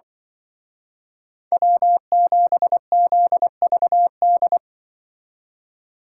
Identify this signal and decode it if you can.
Morse code, heard as a keyed tone, about 24 wpm, 700 Hz. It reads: W7ZVD
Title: Speed Racing - USA 1x3 Call Signs 20wpm